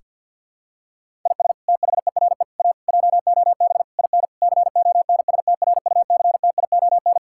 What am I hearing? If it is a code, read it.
Morse code, heard as a keyed tone, about 47 wpm, 700 Hz. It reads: IS THERE A JOB IN CONSTRUCTION
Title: Sentences from Top 1600 Words 50wpm